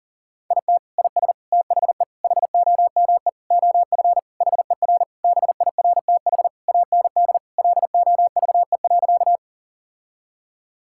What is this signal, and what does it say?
Morse code, heard as a keyed tone, about 40 wpm, 700 Hz. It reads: IT IS THE HOME OF HER BIRTH AND LOVE.
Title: Sentences from Top 1000 Words - Review Part 5 - 40wpm